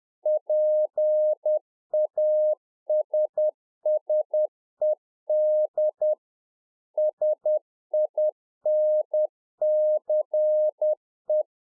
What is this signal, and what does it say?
Morse code, heard as a keyed tone, about 10 wpm, 610 Hz. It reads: PASSED SINCE
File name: 2011-04-mp3 / A SMALL TRANCEIVER 1 10WPM